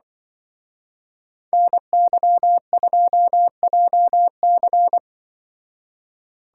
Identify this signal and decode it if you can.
Morse code, heard as a keyed tone, about 24 wpm, 700 Hz. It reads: NY2JC
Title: Speed Racing - USA Call Signs 20wpm